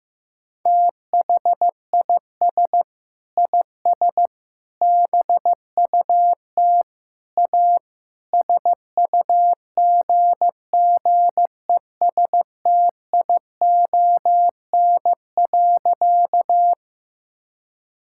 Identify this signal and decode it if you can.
Morse code, heard as a keyed tone, about 14 wpm, 700 Hz. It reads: THIS IS BUT A SUGGESTION.
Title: Sentences from Top 1600 Words 15wpm